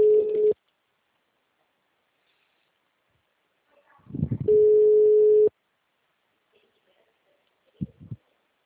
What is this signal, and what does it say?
Ringback tone (Europe, 425 Hz).
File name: out- -2029- - - .34679